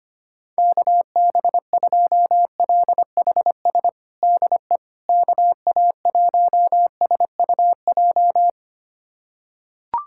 Morse code, heard as a keyed tone, about 25 wpm, 700 Hz. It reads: KB2LHS DE KA1SUJ
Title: USA Callsign DE Callsign 25wpm